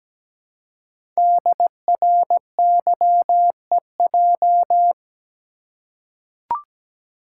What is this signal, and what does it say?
Morse code, heard as a keyed tone, about 17 wpm, 700 Hz. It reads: DRYEJ
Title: Code Groups - 5 Letters - 17wpm